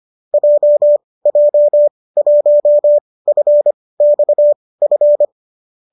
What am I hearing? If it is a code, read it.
Morse code, heard as a keyed tone, about 25 wpm, spaced as for 18 wpm, 590 Hz. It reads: JJ1FXF